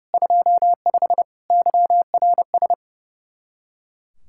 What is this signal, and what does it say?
Morse code, heard as a keyed tone, about 36 wpm, 700 Hz. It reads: 25 YRS
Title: Speed-Racing Common QSO Phrases - 30wpm